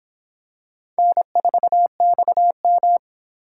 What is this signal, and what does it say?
Morse code, heard as a keyed tone, about 25 wpm, 700 Hz. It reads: N4XM